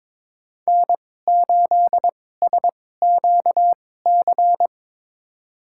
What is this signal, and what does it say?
Morse code, heard as a keyed tone, about 21 wpm, spaced as for 16 wpm, 700 Hz. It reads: N8SQC
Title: USA Call Signs - 5 Characters - 2x Character Spacing - 22wpm